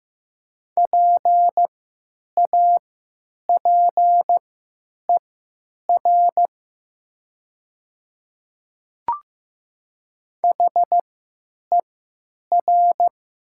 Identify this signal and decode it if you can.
Morse code, heard as a keyed tone, about 15 wpm, spaced as for 9 wpm, 700 Hz. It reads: PAPER HER
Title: 3x Character Spacing - Top 500 Words - 15wpm